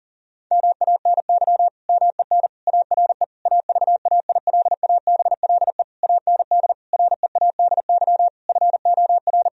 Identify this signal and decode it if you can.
Morse code, heard as a keyed tone, about 40 wpm, 700 Hz. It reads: MANY MEN ARE AVAILABLE AND READY FOR
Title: Sentences from Top 1000 Words - Review Part 3 - 40wpm